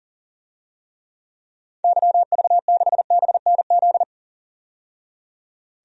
Morse code, heard as a keyed tone, about 40 wpm, 700 Hz. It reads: YV6BNZ
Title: DXCC-100 Call Signs - 6 Characters - 40wpm